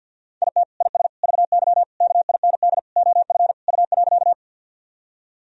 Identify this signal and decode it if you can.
Morse code, heard as a keyed tone, about 51 wpm, 700 Hz. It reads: IT IS VY KIND OF U.
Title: Sentences from Top 200 Words and Abbreviations 50wpm